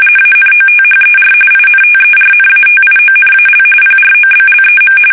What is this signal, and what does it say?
RTTY, about 45.45 baud. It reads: ANOTHER INTERNATIONAL PROBLEM
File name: fsk850 / usnavy2